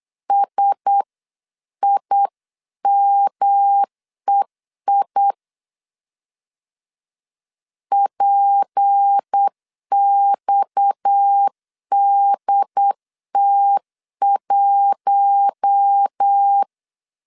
Morse code, heard as a keyed tone, about 8 wpm, 800 Hz. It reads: SIMEI PXDT1
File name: 8wpm